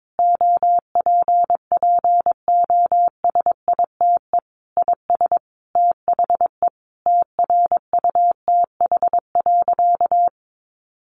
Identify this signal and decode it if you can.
Morse code, heard as a keyed tone, about 22 wpm, 700 Hz. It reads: OPPOSITE IS THE TRUTH.
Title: Sentences from Top 1300 Words 22wpm